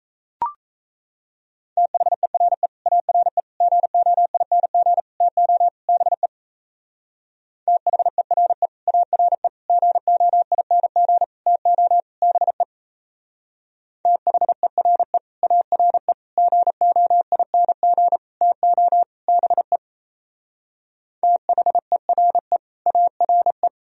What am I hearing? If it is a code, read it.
Morse code, heard as a keyed tone, about 36 wpm, 700 Hz. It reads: THERE ARE GOING TO BE THERE ARE GOING TO BE THERE ARE GOING TO BE THERE ARE